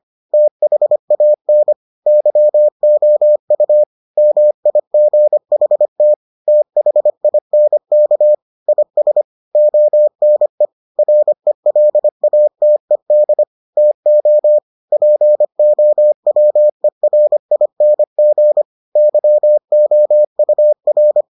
Morse code, heard as a keyed tone, about 25 wpm, 600 Hz. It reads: THAN YOU MIGHT THINK IS ONE RELATED TO POWERING YOUR